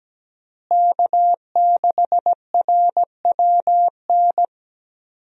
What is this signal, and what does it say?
Morse code, heard as a keyed tone, about 17 wpm, 700 Hz. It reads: K6RWN
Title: USA Call Signs - 5 Characters - Repeated 2x - 17wpm